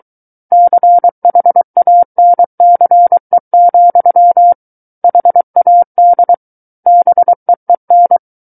Morse code, heard as a keyed tone, about 23 wpm, 700 Hz. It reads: CHANCE, HAD BEEN
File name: Book One / 14-Chapter 14 - 23 WPM